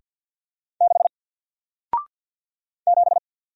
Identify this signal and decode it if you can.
Morse code, heard as a keyed tone, about 50 wpm, 700 Hz. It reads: 6 7
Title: Single Letter-Number 50wpm